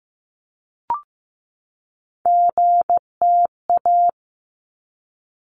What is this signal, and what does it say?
Morse code, heard as a keyed tone, about 15 wpm, 700 Hz. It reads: GTA
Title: ARRL Sections 15wpm